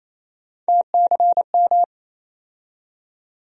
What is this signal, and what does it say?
Morse code, heard as a keyed tone, about 28 wpm, 700 Hz. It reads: TCM
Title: USA Call Sign Postfixes - 3 Letters 28wpm